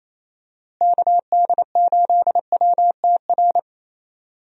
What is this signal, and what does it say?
Morse code, heard as a keyed tone, about 28 wpm, 700 Hz. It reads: KD8WTR
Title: USA Call Signs - 6 Characters - Repeated 3x - 28wpm